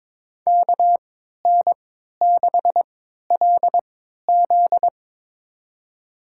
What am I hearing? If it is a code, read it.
Morse code, heard as a keyed tone, about 22 wpm, spaced as for 13 wpm, 700 Hz. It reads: KN6LZ
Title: USA Call Signs - 5 Characters - 3x Character Spacing - 22wpm